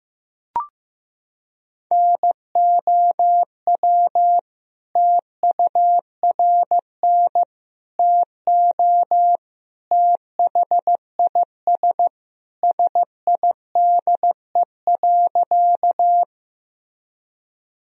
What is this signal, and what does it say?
Morse code, heard as a keyed tone, about 16 wpm, 700 Hz. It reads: NOW TURN TO THIS SIDE.
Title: Sentences from Top 1000 Words - Review Part 1 - 15wpm